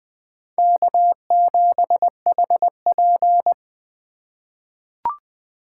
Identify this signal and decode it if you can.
Morse code, heard as a keyed tone, about 20 wpm, 700 Hz. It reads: K7HP
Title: USA Call Signs - 4 Characters - 6x Character Spacing - 20wpm